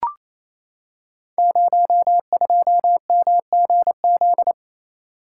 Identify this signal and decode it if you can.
Morse code, heard as a keyed tone, about 28 wpm, 700 Hz. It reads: 02MGZ
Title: Code Groups - 5 Letter:Numbers - 28wpm